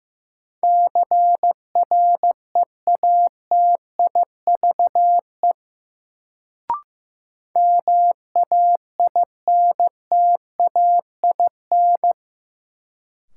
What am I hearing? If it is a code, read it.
Morse code, heard as a keyed tone, about 15 wpm, 700 Hz. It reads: CREATIVE MAINTAIN
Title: Sentences from Top 1000 Words - Review Part 5 - New Word Drill - 15wpm